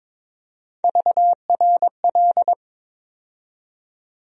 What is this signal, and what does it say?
Morse code, heard as a keyed tone, about 22 wpm, 700 Hz. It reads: VRL
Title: USA Call Sign Postfixes - 3 Letters 22wpm